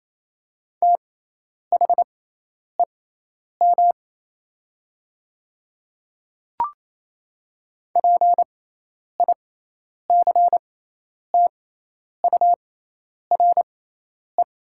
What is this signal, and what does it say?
Morse code, heard as a keyed tone, about 28 wpm, spaced as for 10 wpm, 700 Hz. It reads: THEM PICTURE
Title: 6x Character Spacing - Top 200 Words - 28wpm